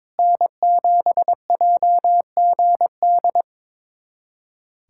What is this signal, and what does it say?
Morse code, heard as a keyed tone, about 21 wpm, 700 Hz. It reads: N7JGD
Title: USA Callsign DE Callsign - 5 Characters - 22wpm